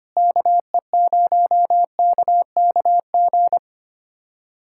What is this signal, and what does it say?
Morse code, heard as a keyed tone, about 25 wpm, 700 Hz. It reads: KE0KKG
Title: USA Callsign DE Callsign - 6 Characters - 25wpm